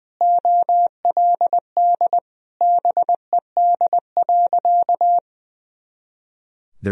Morse code, heard as a keyed tone, about 20 wpm, 700 Hz. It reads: OLD BED.
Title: Sentences from Top 500 Words and Abbreviations 20wpm